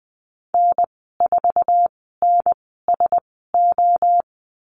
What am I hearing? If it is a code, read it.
Morse code, heard as a keyed tone, about 20 wpm, spaced as for 14 wpm, 700 Hz. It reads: N4NSO
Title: 048 Lesson 9g - Number 4 - Call Signs 20wpm